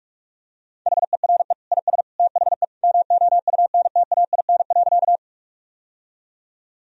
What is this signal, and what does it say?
Morse code, heard as a keyed tone, about 47 wpm, 700 Hz. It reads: HERE IS THE MOUNTAIN.
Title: Sentences from Top 1000 Words - Review Part 1 - 45wpm